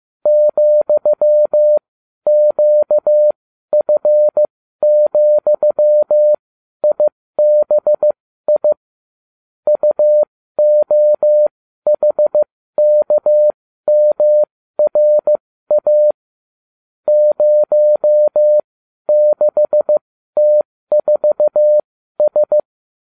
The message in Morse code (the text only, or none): ,QF,IBI UOHKMRA 06T4S